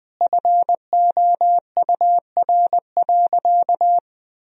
Morse code, heard as a keyed tone, about 20 wpm, 700 Hz. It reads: FOUR.
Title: Sentences from Top 1000 Words - Review Part 1 - 20wpm